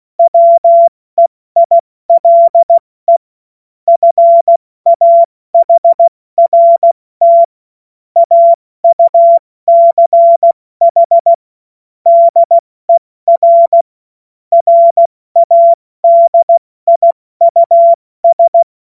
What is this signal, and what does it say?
Morse code, heard as a keyed tone, about 16 wpm, 680 Hz. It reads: WEILE FAHRT AUCH DER RADIUS
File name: Morsekurs / Witze / Witz 21. Juni 2022 70 F80